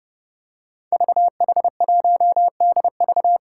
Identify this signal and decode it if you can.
Morse code, heard as a keyed tone, about 30 wpm, 700 Hz. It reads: VH1DV
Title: Five Challenging Letter-Number Combinations 30wpm